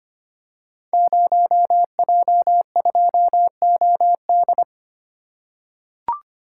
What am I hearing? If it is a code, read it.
Morse code, heard as a keyed tone, about 25 wpm, 700 Hz. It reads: 0J2OD
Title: Code Groups - 5 Letter:Numbers - 25wpm